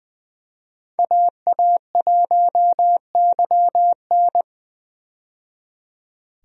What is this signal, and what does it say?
Morse code, heard as a keyed tone, about 20 wpm, 700 Hz. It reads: AA1YN
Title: USA Call Signs - 5 Characters - 20wpm